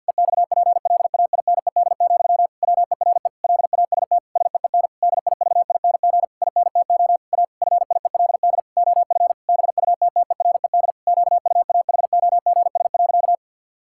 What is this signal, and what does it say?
Morse code, heard as a keyed tone, about 53 wpm, 700 Hz. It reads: EXPLAINED, WERE LAST SEEN DIVING INTO A FIELD OF BUTTERED QUAHOGS.
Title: Pangram Challenge v2 - 50wpm